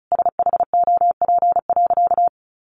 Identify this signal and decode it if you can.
Morse code, heard as a keyed tone, about 34 wpm, 700 Hz. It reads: SHOP.
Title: Additional Easy Phrases 35wpm